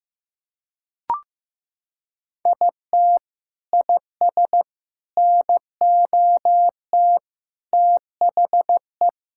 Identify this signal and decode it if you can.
Morse code, heard as a keyed tone, about 15 wpm, 700 Hz. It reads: IT IS NOT THE
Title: Sentences from Top 1000 Words - Review Part 6 - 15wpm